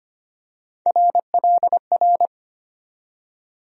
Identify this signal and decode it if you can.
Morse code, heard as a keyed tone, about 25 wpm, 700 Hz. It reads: RLR
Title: USA Call Sign Postfixes - 3 Letters 25wpm